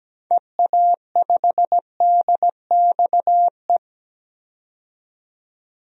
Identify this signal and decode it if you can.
Morse code, heard as a keyed tone, about 17 wpm, 700 Hz. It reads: EA5DXE
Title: DXCC-100 Call Signs - 6 Characters - 17wpm